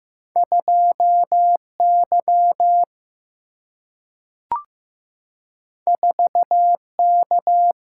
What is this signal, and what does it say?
Morse code, heard as a keyed tone, about 15 wpm, 700 Hz. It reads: 2Y 4K
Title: Single-Letter Single-Number 15wpm